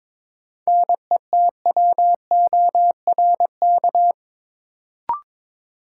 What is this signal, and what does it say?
Morse code, heard as a keyed tone, about 22 wpm, 700 Hz. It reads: NETWORK
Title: Sentences from Top 1000 Words - Review Part 7 - New Word Drill - 22wpm